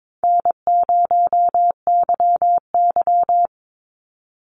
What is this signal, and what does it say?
Morse code, heard as a keyed tone, about 22 wpm, 700 Hz. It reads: N0YY
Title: Easy Call Signs Repeated 3x - 22wpm